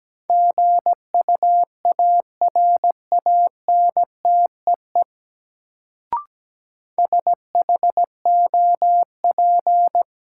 Morse code, heard as a keyed tone, about 17 wpm, 700 Hz. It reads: GUARANTEE SHOP
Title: Sentences from Top 1600 Words - New Word Drill - 17wpm